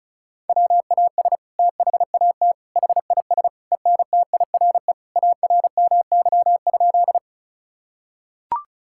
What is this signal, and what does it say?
Morse code, heard as a keyed tone, about 36 wpm, 690 Hz. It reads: WAS THAT HIS ENTIRE ARMY?
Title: Sentences from Top 1000 Words - Review Part 5 - 35wpm